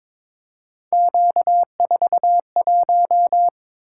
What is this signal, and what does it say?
Morse code, heard as a keyed tone, about 23 wpm, 700 Hz. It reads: Q41
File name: practice / 22wpm / Three Letter-Number Combinations - 22wpm